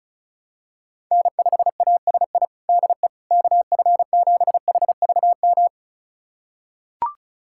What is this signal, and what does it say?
Morse code, heard as a keyed tone, about 36 wpm, 700 Hz. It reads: N5ASI DE KF7HVM
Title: USA Callsign DE Callsign 35wpm